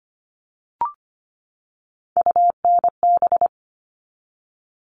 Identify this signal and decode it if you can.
Morse code, heard as a keyed tone, about 25 wpm, 700 Hz. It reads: UNB
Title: All Cut Numbers - 3-Digit 25wpm